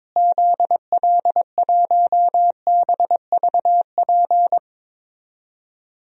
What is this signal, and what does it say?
Morse code, heard as a keyed tone, about 22 wpm, 700 Hz. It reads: ZL1BVP
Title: DXCC-100 Call Signs - 6 Characters - Repeated 2x - 22wpm